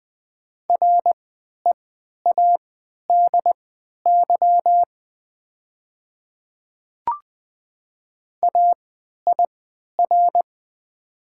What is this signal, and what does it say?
Morse code, heard as a keyed tone, about 20 wpm, spaced as for 11 wpm, 700 Hz. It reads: READY AIR
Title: 3x Character Spacing - Top 500 Words - 20wpm